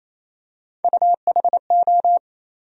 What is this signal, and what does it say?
Morse code, heard as a keyed tone, about 28 wpm, 700 Hz. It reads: UHO
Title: USA Call Sign Postfixes - 3 Letters 28wpm